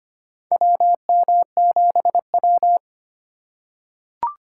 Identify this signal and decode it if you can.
Morse code, heard as a keyed tone, about 25 wpm, 700 Hz. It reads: WM7W
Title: USA Call Signs - 3x Character Spacing - 25wpm